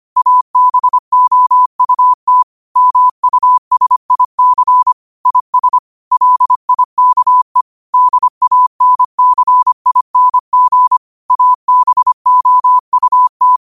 Morse code, heard as a keyed tone, about 25 wpm, 1000 Hz. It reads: ABOUT MUSIC IS LIKE DANCING ABOUT
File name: qotdfiles / 25wpm